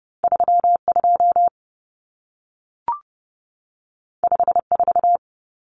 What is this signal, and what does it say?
Morse code, heard as a keyed tone, about 30 wpm, 700 Hz. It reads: 32 54
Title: Numbers - 2-Digit 30wpm